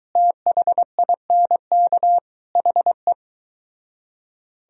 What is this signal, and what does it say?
Morse code, heard as a keyed tone, about 23 wpm, 700 Hz. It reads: THINK HE